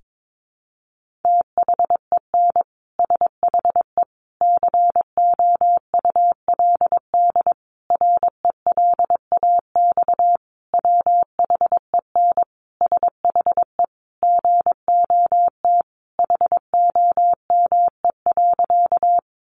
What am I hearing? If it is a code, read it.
Morse code, heard as a keyed tone, about 22 wpm, 700 Hz. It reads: THEN SHE COULD RELAX WHEN SHE GOT HOME.
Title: Sentences from Top 1400 Words 22wpm